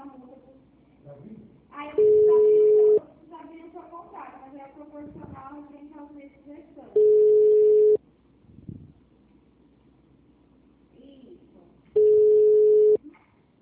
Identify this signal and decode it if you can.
Ringback tone (Europe, 425 Hz), with speech.
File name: out- -2001- - - .528